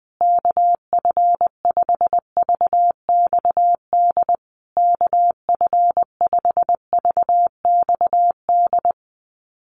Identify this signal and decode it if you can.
Morse code, heard as a keyed tone, about 20 wpm, 700 Hz. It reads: KF5VXD KF5VXD
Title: USA Call Signs Repeated 3x 20wpm